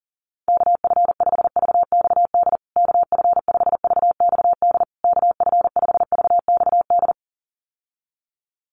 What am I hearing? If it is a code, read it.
Morse code, heard as a keyed tone, about 40 wpm, 700 Hz. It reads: KF5VXD KF5VXD KF5VXD
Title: USA Call Signs Repeated 3x 40wpm